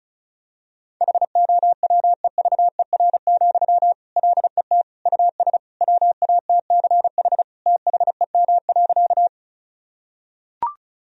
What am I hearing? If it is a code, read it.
Morse code, heard as a keyed tone, about 34 wpm, 700 Hz. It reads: HOWEVER, LET US WATCH THEM.
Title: Sentences from Top 1000 Words - Review Part 1 - 35wpm